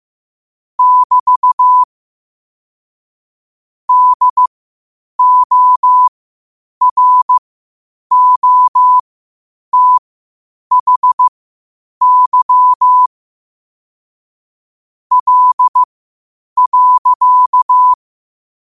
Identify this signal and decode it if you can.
Morse code, heard as a keyed tone, about 15 wpm, spaced as for 8 wpm, 1000 Hz. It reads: <BT> DOROTHY L.
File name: qotdfiles / 8wpm